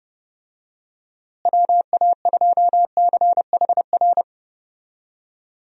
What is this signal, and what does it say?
Morse code, heard as a keyed tone, about 30 wpm, 700 Hz. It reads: WA2CHR
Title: USA Call Signs - 6 Characters - Repeated 2x - 30wpm